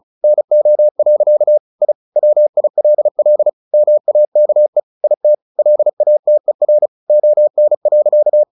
Morse code, heard as a keyed tone, about 34 wpm, 600 Hz. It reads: NO. I WILL MAKE IT LATER ON.
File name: the old man and the sea / the old man and the sea 1 35wpm